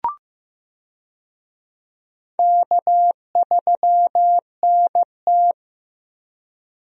Morse code, heard as a keyed tone, about 15 wpm, 700 Hz. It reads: K3NT
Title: USA Call Signs - 4 Characters - 15wpm